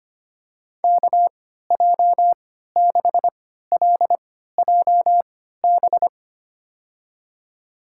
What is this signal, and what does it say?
Morse code, heard as a keyed tone, about 25 wpm, spaced as for 14 wpm, 700 Hz. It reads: KJ6LJB
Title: USA Call Signs - 6 Characters - 3x Character Spacing - 25wpm